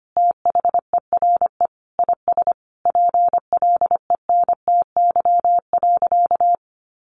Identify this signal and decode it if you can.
Morse code, heard as a keyed tone, about 25 wpm, 700 Hz. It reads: THERE IS PLENTY.
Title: Sentences from Top 1300 Words 25wpm